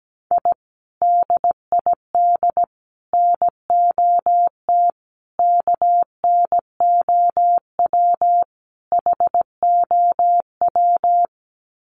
Morse code, heard as a keyed tone, about 17 wpm, 700 Hz. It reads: I DID NOT KNOW HOW
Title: Sets of 5 Words Encore 17wpm